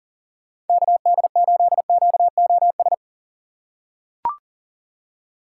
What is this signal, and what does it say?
Morse code, heard as a keyed tone, about 40 wpm, 700 Hz. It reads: KD8QOS
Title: USA Call Signs - 6 Characters - 2x Character Spacing - 40wpm